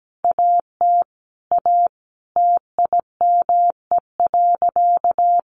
Morse code, heard as a keyed tone, about 17 wpm, 700 Hz. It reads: AT A TIME.
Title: Sentences from Top 1100 Words 17wpm